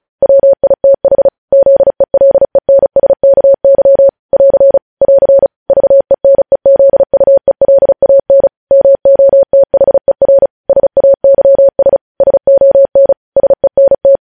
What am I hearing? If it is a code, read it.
Morse code, heard as a keyed tone, about 36 wpm, 550 Hz. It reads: WITH ZELENSKY <AR> <AR> VENEZUELAN MOTHER SAYS SON SENT
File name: podcast / News headlines-1x-35x35